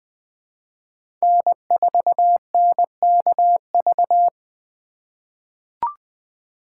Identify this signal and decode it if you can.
Morse code, heard as a keyed tone, about 20 wpm, 700 Hz. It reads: N4NKV
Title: USA Call Signs - 4x Character Spacing - 20wpm